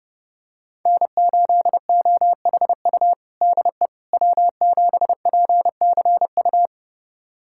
Morse code, heard as a keyed tone, about 30 wpm, 700 Hz. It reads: N8OHU DE W7PCU
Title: USA Callsign DE Callsign - 5 Characters - 30wpm